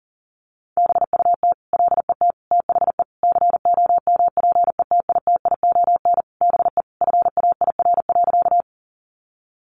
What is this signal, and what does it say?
Morse code, heard as a keyed tone, about 37 wpm, 700 Hz. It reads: BUT LET THE COMPETITION BE FAIR.
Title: Sentences from Top 1300 Words 40wpm